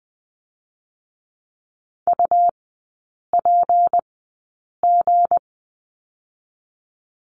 Morse code, heard as a keyed tone, about 20 wpm, spaced as for 8 wpm, 700 Hz. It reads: UPG
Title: 218 Lesson 35e - Number 0 - Three Characters 20wpm